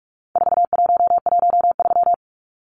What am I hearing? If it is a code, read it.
Morse code, heard as a keyed tone, about 44 wpm, 700 Hz. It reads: 4113
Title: Numbers - 4-Digit 45wpm